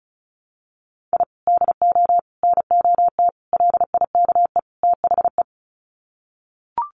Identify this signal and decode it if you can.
Morse code, heard as a keyed tone, about 34 wpm, 700 Hz. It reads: I DO NOT LIKE THE
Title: Sets of 5 Words Encore 35wpm